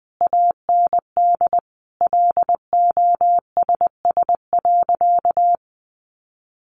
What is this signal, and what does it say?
Morse code, heard as a keyed tone, about 20 wpm, 700 Hz. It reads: AND LOSS.
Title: Sentences from Top 1400 Words 20wpm